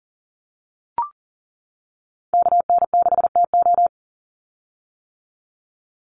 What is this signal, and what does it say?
Morse code, heard as a keyed tone, about 40 wpm, 700 Hz. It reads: KN6TO
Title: USA Call Signs 40wpm